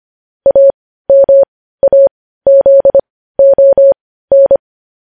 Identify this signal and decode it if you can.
Morse code, heard as a keyed tone, about 25 wpm, spaced as for 15 wpm, 550 Hz. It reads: AMAZON
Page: news headlines in morse code at 15 wpm